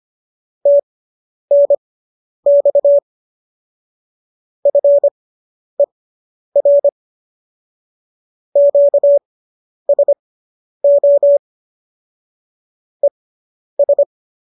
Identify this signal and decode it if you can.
Morse code, heard as a keyed tone, about 25 wpm, spaced as for 10 wpm, 570 Hz. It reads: TNX FER QSO ES